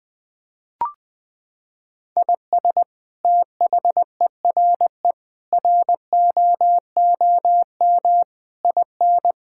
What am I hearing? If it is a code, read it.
Morse code, heard as a keyed tone, about 20 wpm, 700 Hz. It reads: IS THERE ROOM IN
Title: Sentences from Top 1000 Words - Review Part 5 - 20wpm